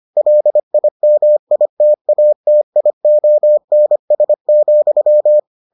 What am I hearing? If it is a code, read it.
Morse code, heard as a keyed tone, about 25 wpm, 600 Hz. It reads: LIMITATIONS,